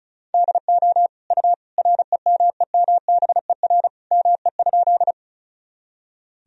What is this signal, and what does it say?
Morse code, heard as a keyed tone, about 34 wpm, 700 Hz. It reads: DO U REMEMBER ME?
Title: Sentences from Top 400 Words and Abbreviations 35wpm